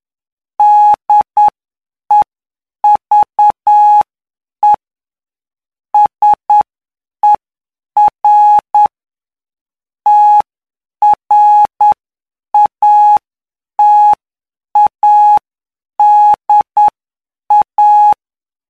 Morse code, heard as a keyed tone, about 9 wpm, spaced as for 7 wpm, 840 Hz. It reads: DEVE SER TRATADA
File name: cw / py2lcd / 18-PY2LCD-Li��o 27